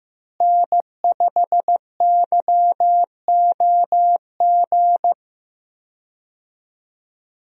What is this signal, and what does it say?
Morse code, heard as a keyed tone, about 15 wpm, 700 Hz. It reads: N5YOG
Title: USA Call Signs - 5 Characters - Repeated 2x - 15wpm